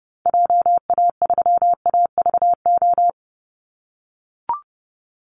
Morse code, heard as a keyed tone, about 28 wpm, 700 Hz. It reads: JA3AVO
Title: Call Signs - 30wpm